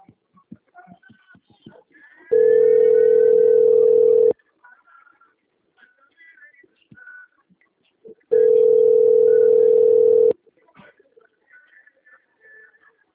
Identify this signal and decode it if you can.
Ringback tone (North America), 440 + 480 Hz.